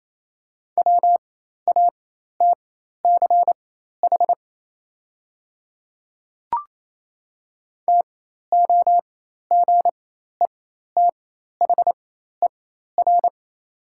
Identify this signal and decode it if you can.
Morse code, heard as a keyed tone, about 28 wpm, spaced as for 13 wpm, 700 Hz. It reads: WATCH TOGETHER
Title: 4x Character Spacing - Top 300 Words - 28wpm